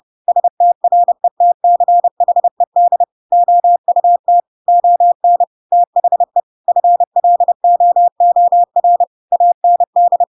Morse code, heard as a keyed tone, about 30 wpm, 700 Hz. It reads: STRETCHED OUT ON THE FLOOR AND
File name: Pinocchio / Pinocchio ch7 30 30 ik1hge